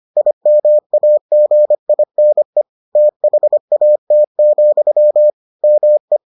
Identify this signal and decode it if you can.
Morse code, heard as a keyed tone, about 25 wpm, 600 Hz. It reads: IMAGINE THAT, ME